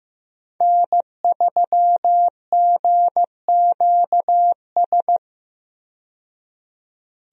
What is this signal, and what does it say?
Morse code, heard as a keyed tone, about 15 wpm, 700 Hz. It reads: N3GQS
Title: USA Call Signs - 5 Characters - 15wpm